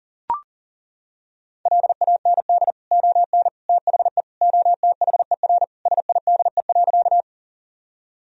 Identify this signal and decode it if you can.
Morse code, heard as a keyed tone, about 37 wpm, 700 Hz. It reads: LAND ON THE OTHER SIDE.
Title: Sentences from Top 1000 Words - Review Part 1 - 40wpm